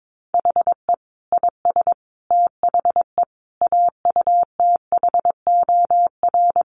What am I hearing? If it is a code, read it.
Morse code, heard as a keyed tone, about 22 wpm, 700 Hz. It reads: HE IS THE AUTHOR
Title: Sentences from Top 1200 Words 22wpm